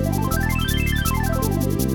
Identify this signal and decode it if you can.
There is no code.